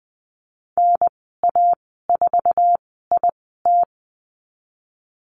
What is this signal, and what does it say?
Morse code, heard as a keyed tone, about 20 wpm, 700 Hz. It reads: N A 4 I T
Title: 048 Lesson 9g - Number 4 - Call Signs 20wpm